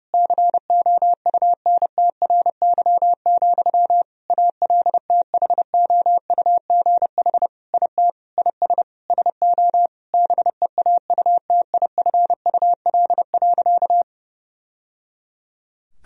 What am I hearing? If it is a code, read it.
Morse code, heard as a keyed tone, about 30 wpm, 700 Hz. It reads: COUNTRY, ALTHOUGH IT IS SO BEAUTIFUL.
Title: Sentences from Top 1000 Words - Review Part 3 - 30wpm